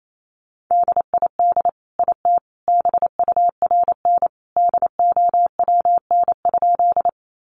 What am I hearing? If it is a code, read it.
Morse code, heard as a keyed tone, about 28 wpm, 700 Hz. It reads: DID IT BURN DOWN?
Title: Sentences from Top 1300 Words 28wpm